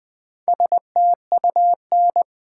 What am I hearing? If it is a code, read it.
Morse code, heard as a keyed tone, about 20 wpm, 690 Hz. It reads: STUN
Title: Build A Word - 2 Letters At A Time - 4-Letter Words - 20wpm